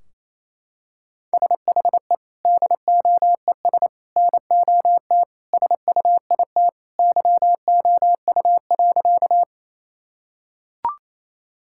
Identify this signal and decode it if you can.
Morse code, heard as a keyed tone, about 28 wpm, 700 Hz. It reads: SHE DOES NOT SUIT YOU.
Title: Sentences from Top 1600 Words 28wpm